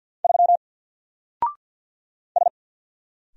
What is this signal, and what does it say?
Morse code, heard as a keyed tone, about 50 wpm, 700 Hz. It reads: 3 S
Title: Single Letter-Number Alt 50wpm